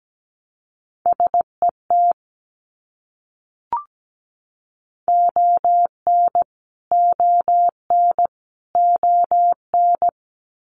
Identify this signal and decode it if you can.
Morse code, heard as a keyed tone, about 17 wpm, 700 Hz. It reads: SET ON ON ON
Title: Top 300 Words Repeated 17wpm